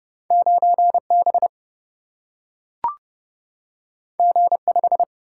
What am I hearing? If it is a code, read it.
Morse code, heard as a keyed tone, about 30 wpm, 700 Hz. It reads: 9B G5
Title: Single-Letter Single-Number 30wpm